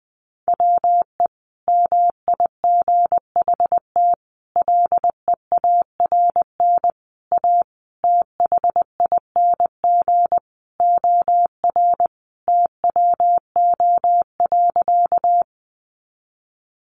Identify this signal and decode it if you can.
Morse code, heard as a keyed tone, about 20 wpm, 700 Hz. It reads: WE MIGHT LEARN A THING OR TWO.
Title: Sentences from Top 300 Words 20wpm